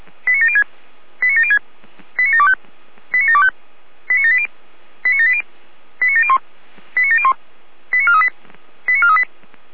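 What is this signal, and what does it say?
Five-tone selective call (ZVEI), heard as quick runs of five tones, tones of about 80 ms. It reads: 89706 89706 89725 89725 89780 89780 89701 89701 89438 89438
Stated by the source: ZVEI five-tone call received by SDR, from feuerwehr-orlamuende.org audio archive.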